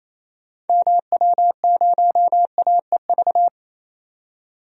Morse code, heard as a keyed tone, about 28 wpm, 700 Hz. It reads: MW0AEV
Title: DXCC-100 Call Signs - 6 Characters - 28wpm